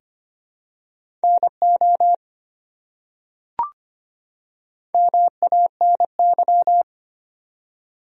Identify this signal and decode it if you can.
Morse code, heard as a keyed tone, about 25 wpm, 700 Hz. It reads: NO MANY
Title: Top 100 Words 25wpm